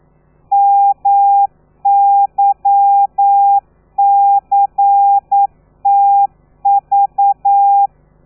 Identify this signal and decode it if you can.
Morse code, heard as a keyed tone, about 9 wpm, 800 Hz. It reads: MYCTV
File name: telegrafia / i2tfj / Lez13 40